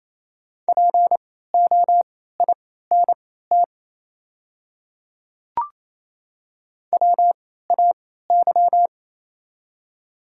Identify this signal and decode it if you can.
Morse code, heard as a keyed tone, about 28 wpm, spaced as for 16 wpm, 700 Hz. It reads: POINT WAY
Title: 3x Character Spacing - Top 300 Words - 28wpm